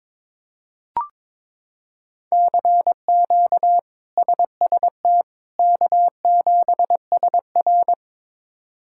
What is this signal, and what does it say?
Morse code, heard as a keyed tone, about 21 wpm, 700 Hz. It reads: CQ SST K7SR
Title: K1USN SST QSO Practice - 22wpm